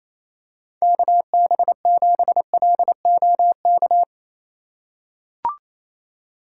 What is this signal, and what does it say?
Morse code, heard as a keyed tone, about 28 wpm, 700 Hz. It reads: KB7LOK
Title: USA Call Signs - 6 Characters - 4x Character Spacing - 28wpm